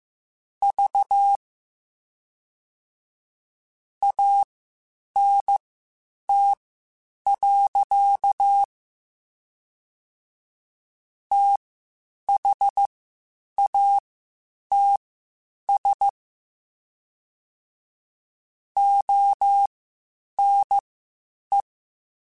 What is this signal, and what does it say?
Morse code, heard as a keyed tone, about 15 wpm, spaced as for 8 wpm, 800 Hz. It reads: V ANT. THATS ONE
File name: QSOs / QSOs-mp3s / WD0FBT 1@13